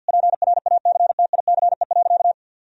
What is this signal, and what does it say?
Morse code, heard as a keyed tone, about 52 wpm, 700 Hz. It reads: PRACTICE.
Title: Sentences from Top 1000 Words - Review Part 4 - 50wpm